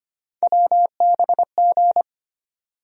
Morse code, heard as a keyed tone, about 25 wpm, 700 Hz. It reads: WBG
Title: USA Call Sign Postfixes - 3 Letters 25wpm